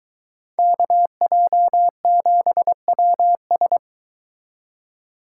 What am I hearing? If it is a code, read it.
Morse code, heard as a keyed tone, about 22 wpm, 700 Hz. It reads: KJ7WS